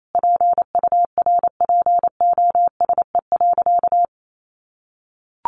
Morse code, heard as a keyed tone, about 28 wpm, 700 Hz. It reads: PURPOSE.
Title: Sentences from Top 1400 Words 28wpm